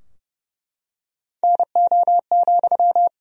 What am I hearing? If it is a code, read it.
Morse code, heard as a keyed tone, about 30 wpm, 700 Hz. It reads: NO,
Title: Sentences from Top 1000 Words - Review Part 4 - 30wpm